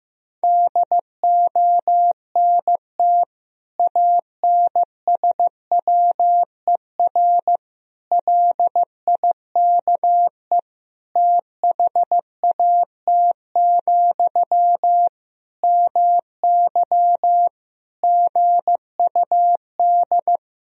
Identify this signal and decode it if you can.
Morse code, heard as a keyed tone, about 15 wpm, 700 Hz. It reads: DONT ANSWER LIKE THAT, MY GUD
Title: Sentences from Top 400 Words and Abbreviations 15wpm